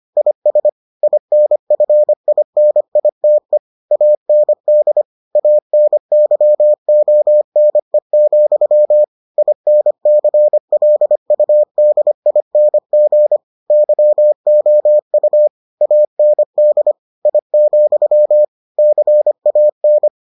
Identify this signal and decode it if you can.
Morse code, heard as a keyed tone, about 25 wpm, 600 Hz. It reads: IS INFINITE AND ANYONE, INCLUDING YOU AND I, CAN